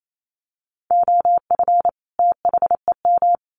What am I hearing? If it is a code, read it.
Morse code, heard as a keyed tone, about 28 wpm, 700 Hz. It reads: OF THEM
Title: Sets of 2 Words 28wpm